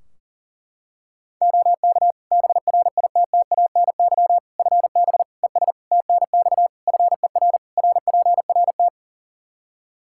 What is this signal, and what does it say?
Morse code, heard as a keyed tone, about 40 wpm, 700 Hz. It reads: OK BRITTANY FB ES TNX FER RPRT
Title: Practice QSO - Exchange 2 - Info - 40wpm